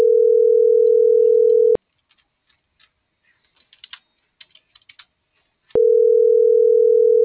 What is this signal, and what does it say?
Ringback tone (North America), 440 + 480 Hz.